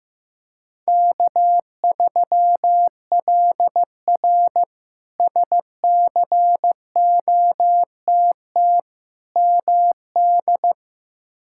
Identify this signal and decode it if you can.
Morse code, heard as a keyed tone, about 15 wpm, 700 Hz. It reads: K3LR SCOTT MD
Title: K1USN SST QSO Practice - 15wpm